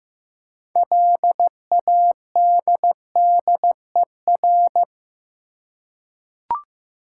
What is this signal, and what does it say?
Morse code, heard as a keyed tone, about 15 wpm, 700 Hz. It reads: LADDER
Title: Six Letter Words - 15wpm